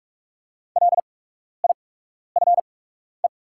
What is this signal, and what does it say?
Morse code, heard as a keyed tone, about 48 wpm, spaced as for 12 wpm, 690 Hz. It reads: LIFE
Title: 8x Character Spacing - Top 300 Words - 45wpm